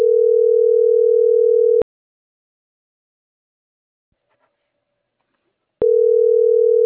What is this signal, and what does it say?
Ringback tone (North America), 440 + 480 Hz.